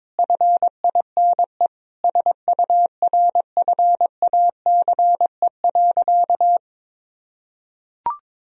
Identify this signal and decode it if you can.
Morse code, heard as a keyed tone, about 22 wpm, 690 Hz. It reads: FINE SURFACE.
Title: Sentences from Top 1000 Words - Review Part 3 - 22wpm